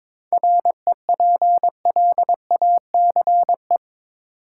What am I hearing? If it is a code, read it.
Morse code, heard as a keyed tone, about 22 wpm, 700 Hz. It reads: REPLACE
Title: Seven Letter Words - 22wpm